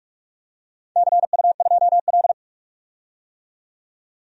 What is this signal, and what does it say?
Morse code, heard as a keyed tone, about 44 wpm, 700 Hz. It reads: CU2L
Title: DXCC-100 Call Signs - 4 Characters - 45wpm